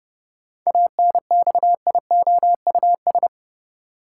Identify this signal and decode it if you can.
Morse code, heard as a keyed tone, about 30 wpm, 700 Hz. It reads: ANXIOUS
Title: Seven Letter Words - 30wpm